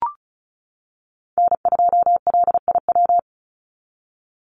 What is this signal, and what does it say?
Morse code, heard as a keyed tone, about 34 wpm, 700 Hz. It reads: N2LIW
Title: USA Call Signs 35wpm